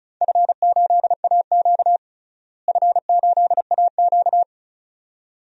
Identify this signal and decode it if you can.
Morse code, heard as a keyed tone, about 36 wpm, 700 Hz. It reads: F8AQ F8AQ
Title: DXCC-100 Call Signs - 4 Characters - Repeated 2x - 35wpm